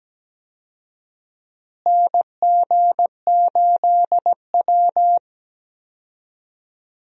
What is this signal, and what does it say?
Morse code, heard as a keyed tone, about 17 wpm, 700 Hz. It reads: NG8W
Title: USA Call Signs - 4 Characters - Repeated 2x - 17wpm